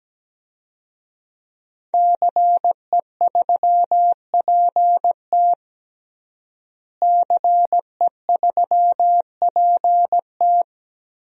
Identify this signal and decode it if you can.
Morse code, heard as a keyed tone, about 17 wpm, 700 Hz. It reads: CE3PT CE3PT
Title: DXCC-100 Call Signs - 5 Characters - Repeated 2x - 17wpm